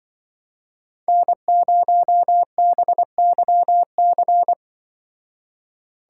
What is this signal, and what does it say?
Morse code, heard as a keyed tone, about 24 wpm, 700 Hz. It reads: N0BYC
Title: Speed Racing - USA 1x3 Call Signs 20wpm